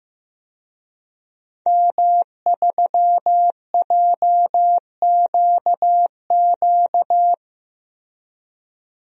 Morse code, heard as a keyed tone, about 16 wpm, 700 Hz. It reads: M3JQQ
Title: DXCC-100 Call Signs - 5 Characters - Repeated 3x - 15wpm